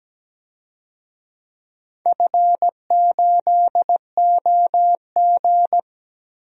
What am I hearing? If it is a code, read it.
Morse code, heard as a keyed tone, about 17 wpm, 700 Hz. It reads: F8OG
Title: DXCC-100 Call Signs - 4 Characters - 17wpm